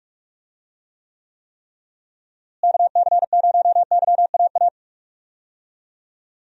Morse code, heard as a keyed tone, about 46 wpm, 700 Hz. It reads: KC0YAA
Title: USA Call Signs - 6 Characters - Repeated 3x - 45wpm